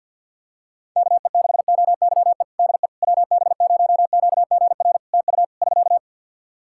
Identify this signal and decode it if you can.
Morse code, heard as a keyed tone, about 49 wpm, 700 Hz. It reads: KE6OYE DE WB0QGR TU <SK>
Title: Practice QSO - Exchange 4 - Ending - 50wpm